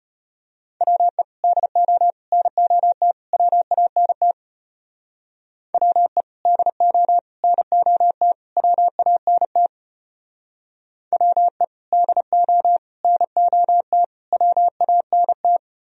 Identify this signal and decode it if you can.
Morse code, heard as a keyed tone, about 33 wpm, 700 Hz. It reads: WE DO NOT WANT WE DO NOT WANT WE DO NOT WANT